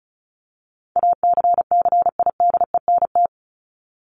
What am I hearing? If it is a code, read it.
Morse code, heard as a keyed tone, about 34 wpm, 700 Hz. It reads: ACCIDENT
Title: Top 1751-2000 Words 35wpm